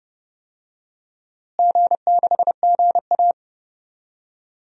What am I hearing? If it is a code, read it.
Morse code, heard as a keyed tone, about 30 wpm, 700 Hz. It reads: G6GA
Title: DXCC-100 Call Signs - 4 Characters - Repeated 3x - 30wpm